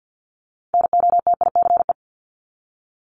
Morse code, heard as a keyed tone, about 50 wpm, 700 Hz. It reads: NOTICE
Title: Top 500 Words 50wpm